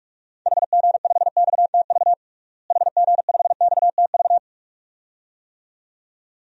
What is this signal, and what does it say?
Morse code, heard as a keyed tone, about 46 wpm, 700 Hz. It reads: HG5XTV HG5XTV
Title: DXCC-100 Call Signs - 6 Characters - Repeated 2x - 45wpm